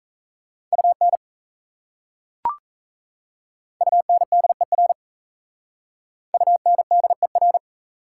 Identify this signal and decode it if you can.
Morse code, heard as a keyed tone, about 36 wpm, 690 Hz. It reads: UN UNDER UNDER